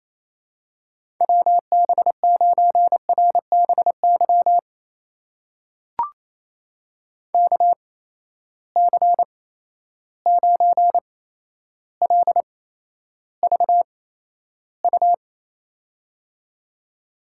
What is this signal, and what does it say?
Morse code, heard as a keyed tone, about 28 wpm, spaced as for 12 wpm, 700 Hz. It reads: WB9RBY KC9LVU
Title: USA Call Signs - 6 Characters - 8x Character Spacing - 28wpm